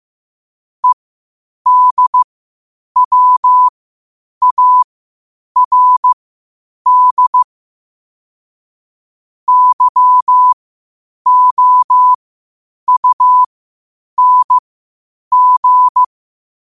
Morse code, heard as a keyed tone, about 15 wpm, spaced as for 8 wpm, 1000 Hz. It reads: EDWARD YOUNG
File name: qotdfiles / 8wpm